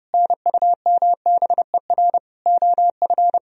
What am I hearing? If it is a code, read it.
Morse code, heard as a keyed tone, about 38 wpm, 700 Hz. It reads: NUMBER OF